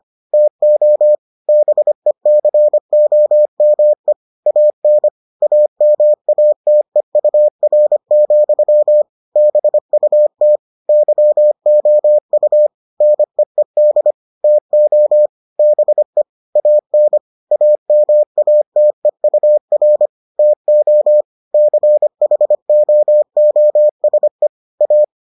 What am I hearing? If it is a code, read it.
Morse code, heard as a keyed tone, about 25 wpm, 600 Hz. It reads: TO BECOME AN AMATEUR, BUT YOU NEED TO BE AN AMATEUR TO CHOOSE A